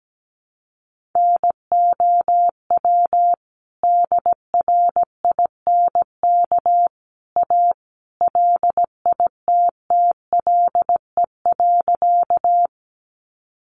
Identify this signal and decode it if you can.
Morse code, heard as a keyed tone, about 17 wpm, 700 Hz. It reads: NOW DRINK A LITTLE.
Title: Sentences from Top 1500 Words 17wpm